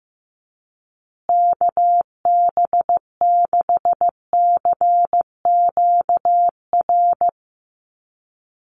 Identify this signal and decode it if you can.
Morse code, heard as a keyed tone, about 15 wpm, 700 Hz. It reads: KB6CQR
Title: USA Call Signs Repeated 3x 15wpm